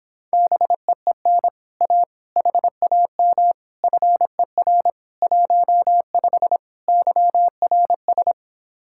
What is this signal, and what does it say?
Morse code, heard as a keyed tone, about 26 wpm, 700 Hz. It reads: BEEN A HAM FER 15 YRS